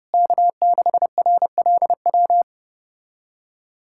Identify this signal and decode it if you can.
Morse code, heard as a keyed tone, about 32 wpm, 700 Hz. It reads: K6RLW
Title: USA Callsign DE Callsign - 5 Characters - 30wpm